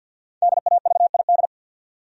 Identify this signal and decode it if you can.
Morse code, heard as a keyed tone, about 52 wpm, 700 Hz. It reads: DAVID
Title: USA Common Names - Max 5 Letters - 50wpm